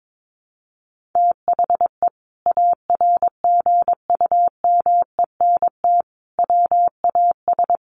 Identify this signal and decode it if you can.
Morse code, heard as a keyed tone, about 22 wpm, 700 Hz. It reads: THE ARGUMENT WAS
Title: Sentences from Top 1400 Words 22wpm